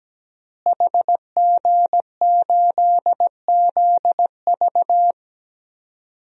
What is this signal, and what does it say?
Morse code, heard as a keyed tone, about 17 wpm, 700 Hz. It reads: HG8ZV
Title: DXCC-100 Call Signs - 5 Characters - Repeated 3x - 17wpm